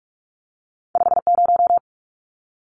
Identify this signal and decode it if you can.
Morse code, heard as a keyed tone, about 46 wpm, 700 Hz. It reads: 50
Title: Numbers - 2-Digit 45wpm